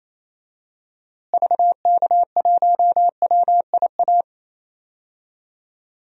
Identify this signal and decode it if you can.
Morse code, heard as a keyed tone, about 28 wpm, 700 Hz. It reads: VK1WIA
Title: DXCC-100 Call Signs - 6 Characters - Repeated 3x - 28wpm